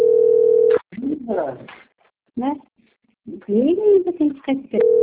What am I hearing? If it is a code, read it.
Ringback tone (North America), 440 + 480 Hz, with speech.
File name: rg-1600-unknown- - - .11209